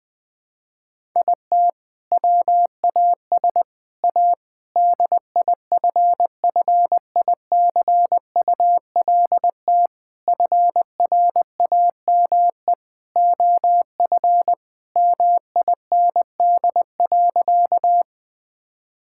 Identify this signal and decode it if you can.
Morse code, heard as a keyed tone, about 21 wpm, 700 Hz. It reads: IT WAS A DIFFICULT FRAME OF MIND.
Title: Sentences from Top 1000 Words - Review Part 6 - 20wpm